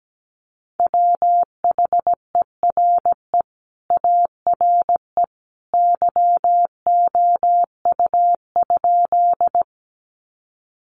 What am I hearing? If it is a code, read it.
Morse code, heard as a keyed tone, about 17 wpm, 700 Hz. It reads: WHERE ARE YOU?
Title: Sentences from Top 200 Words 17wpm